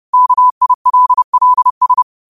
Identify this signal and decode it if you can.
Morse code, heard as a keyed tone, about 30 wpm, 1000 Hz. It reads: KILLS
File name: qotdfiles / 30wpm